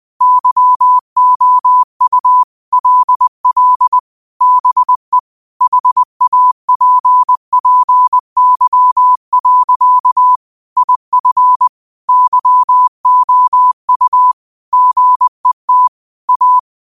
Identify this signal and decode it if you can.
Morse code, heard as a keyed tone, about 20 wpm, 1000 Hz. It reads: YOU LL BE HAPPY. IF YOU GET A